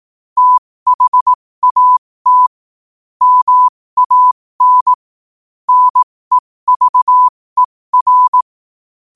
Morse code, heard as a keyed tone, about 18 wpm, spaced as for 15 wpm, 1000 Hz. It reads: THAT MAN NEVER
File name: qotdfiles / 15wpm